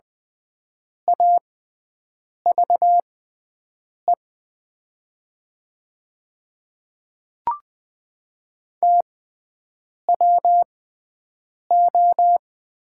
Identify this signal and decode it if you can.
Morse code, heard as a keyed tone, about 20 wpm, spaced as for 7 wpm, 700 Hz. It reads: AVE TWO
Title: 6x Character Spacing - Top 100 Words - 20wpm